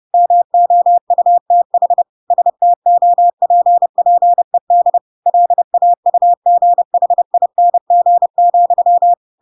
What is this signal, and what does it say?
Morse code, heard as a keyed tone, about 30 wpm, 700 Hz. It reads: MOUTH STOPPED LAUGHING,
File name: Pinocchio / Pinocchio ch3 30 30 ik1hge